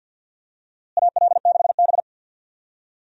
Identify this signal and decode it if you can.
Morse code, heard as a keyed tone, about 46 wpm, 700 Hz. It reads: AL6B
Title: USA Call Signs - 4 Characters - Repeated 3x - 50wpm